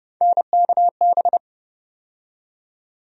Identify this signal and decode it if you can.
Morse code, heard as a keyed tone, about 30 wpm, 700 Hz. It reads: NKB
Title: USA Call Sign Postfixes - 3 Letters 30wpm